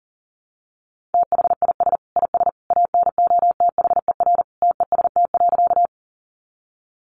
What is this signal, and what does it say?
Morse code, heard as a keyed tone, about 40 wpm, 700 Hz. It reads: THIS IS ANOTHER TEST.
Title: Sentences from Top 600 Words 40wpm